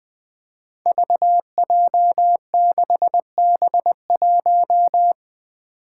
Morse code, heard as a keyed tone, about 20 wpm, 700 Hz. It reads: VJ6B1
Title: Five Challenging Letter-Number Combinations 20wpm